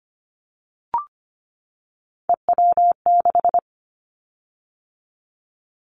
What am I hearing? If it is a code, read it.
Morse code, heard as a keyed tone, about 25 wpm, 700 Hz. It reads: EW6
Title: Callsign Prefixes for Top 50 DXCC - 25wpm